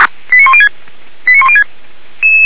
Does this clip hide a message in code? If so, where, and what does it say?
Yes: five-tone selective call (ZVEI) from 0.3 to 1.6 s, heard as quick runs of five tones, tones of about 70 ms. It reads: 89106 89106